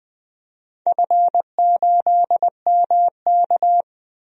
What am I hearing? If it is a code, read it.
Morse code, heard as a keyed tone, about 20 wpm, 700 Hz. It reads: F8MK
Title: DXCC-100 Call Signs - 4 Characters - 20wpm